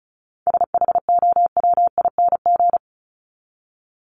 Morse code, heard as a keyed tone, about 34 wpm, 700 Hz. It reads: SHOWING
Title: ING Suffix - Top 100 Words with Base Words 35wpm